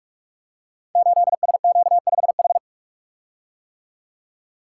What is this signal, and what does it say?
Morse code, heard as a keyed tone, about 44 wpm, 700 Hz. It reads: 8SQ5H
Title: Code Groups - 5 Letter:Numbers - 45wpm